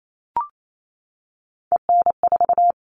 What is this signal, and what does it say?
Morse code, heard as a keyed tone, about 28 wpm, 700 Hz. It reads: EN4
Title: All Cut Numbers - 3-Digit 28wpm